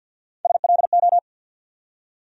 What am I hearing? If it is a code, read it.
Morse code, heard as a keyed tone, about 50 wpm, 700 Hz. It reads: SLO
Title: USA Call Sign Postfixes - 3 Letters 50wpm